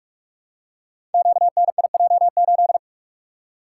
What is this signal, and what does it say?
Morse code, heard as a keyed tone, about 44 wpm, 700 Hz. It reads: QNIJ8
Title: Code Groups - 5 Letter:Numbers - 45wpm